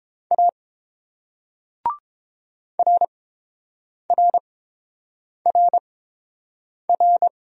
Morse code, heard as a keyed tone, about 30 wpm, spaced as for 7 wpm, 700 Hz. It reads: A RRRR